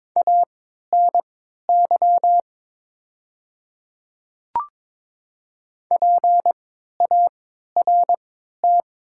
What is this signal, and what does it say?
Morse code, heard as a keyed tone, about 22 wpm, spaced as for 12 wpm, 690 Hz. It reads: ANY PART
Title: 3x Character Spacing - Top 200 Words - 22wpm